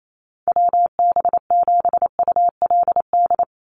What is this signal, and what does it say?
Morse code, heard as a keyed tone, about 28 wpm, 700 Hz. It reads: WB7ULD
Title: NA SOTA Chaser Callsigns 28wpm